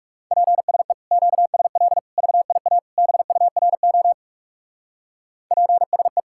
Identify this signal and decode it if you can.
Morse code, heard as a keyed tone, about 36 wpm, 700 Hz. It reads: PSE QSL VIA BURO PSE
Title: Speed-Racing Common QSO Phrases - 30wpm